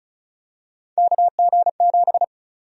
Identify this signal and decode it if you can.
Morse code, heard as a keyed tone, about 36 wpm, 700 Hz. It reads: KG7
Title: USA Call Sign Prefixes - Repeated 2x - 35wpm